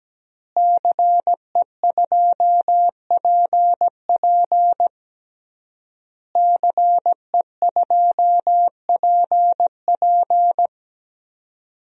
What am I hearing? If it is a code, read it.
Morse code, heard as a keyed tone, about 17 wpm, 700 Hz. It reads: CE2PP CE2PP
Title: DXCC-100 Call Signs - 5 Characters - Repeated 3x - 17wpm